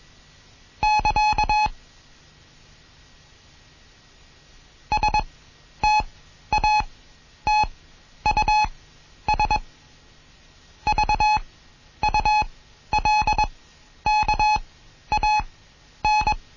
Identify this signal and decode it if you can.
Morse code, heard as a keyed tone, about 21 wpm, spaced as for 11 wpm, 850 Hz. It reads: <KA> STATUS VULKAN